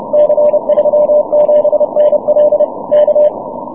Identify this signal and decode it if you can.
Morse code, heard as a keyed tone, about 30 wpm, 600 Hz. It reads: C37NL K
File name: wp / c37nl 6m cw